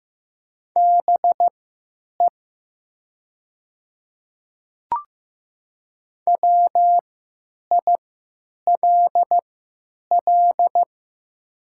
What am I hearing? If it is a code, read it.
Morse code, heard as a keyed tone, about 15 wpm, spaced as for 9 wpm, 690 Hz. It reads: BE WILL
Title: 3x Character Spacing - Top 100 Words - 15wpm